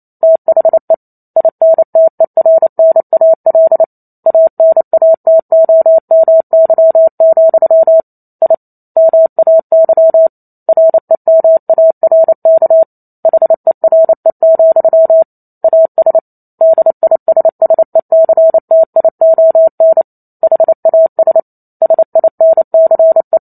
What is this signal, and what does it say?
Morse code, heard as a keyed tone, about 28 wpm, 650 Hz. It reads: THE INTERNAL ANATOMY, I MAY REMARK HERE, AS DISSECTION HAS SINCE